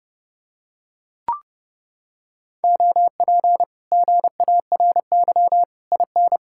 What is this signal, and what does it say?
Morse code, heard as a keyed tone, about 28 wpm, 700 Hz. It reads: OP GARY IN